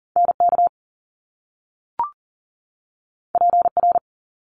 Morse code, heard as a keyed tone, about 40 wpm, 700 Hz. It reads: NK PR
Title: Top Two Letter Combinations in Words 40wpm